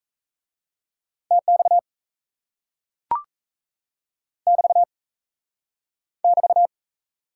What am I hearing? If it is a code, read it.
Morse code, heard as a keyed tone, about 38 wpm, 700 Hz. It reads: TX <BT> <BT>